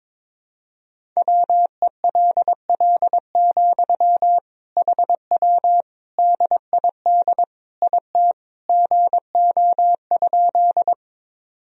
Morse code, heard as a keyed tone, about 22 wpm, 700 Hz. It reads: WELL, HW DID IT GO?
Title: Sentences from Top 200 Words and Abbreviations 22wpm